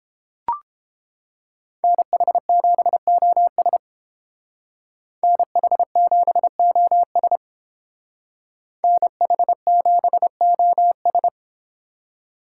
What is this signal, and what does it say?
Morse code, heard as a keyed tone, about 30 wpm, 700 Hz. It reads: NH7OS NH7OS NH7OS